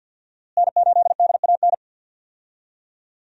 Morse code, heard as a keyed tone, about 49 wpm, 700 Hz. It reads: N8DAN
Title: USA Call Signs - 5 Characters - Repeated 2x - 50wpm